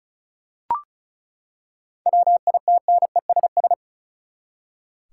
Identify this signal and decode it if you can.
Morse code, heard as a keyed tone, about 36 wpm, 700 Hz. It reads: WITNESS
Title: Seven Letter Words - 35wpm